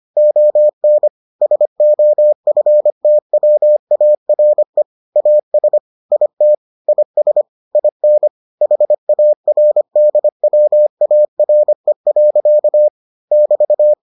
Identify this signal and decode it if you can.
Morse code, heard as a keyed tone, about 25 wpm, 600 Hz. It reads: ON SOFTWARE AS IT IS IN HARDWARE. <BT>